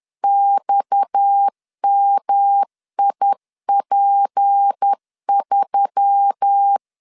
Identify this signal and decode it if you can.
Morse code, heard as a keyed tone, about 10 wpm, 800 Hz. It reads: XMIP3